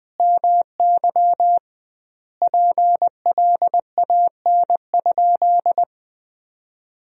Morse code, heard as a keyed tone, about 19 wpm, 700 Hz. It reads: MY PLAN?
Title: 2x Word-Spacing - Sentences from Top 500 Words - 20wpm